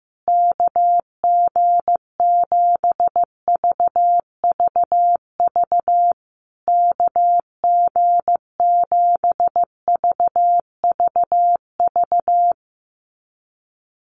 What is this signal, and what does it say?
Morse code, heard as a keyed tone, about 15 wpm, 700 Hz. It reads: KG7VVV KG7VVV
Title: USA Call Signs Repeated 3x 15wpm